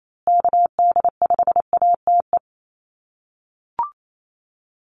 Morse code, heard as a keyed tone, about 28 wpm, 700 Hz. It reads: KD5ATE
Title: USA Call Signs Repeated 2x 28wpm